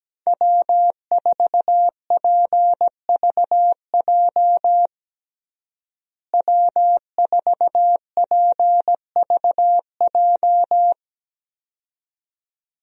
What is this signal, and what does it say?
Morse code, heard as a keyed tone, about 17 wpm, 700 Hz. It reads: W4PVJ W4PVJ
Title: USA Call Signs - 5 Characters - Repeated 2x - 17wpm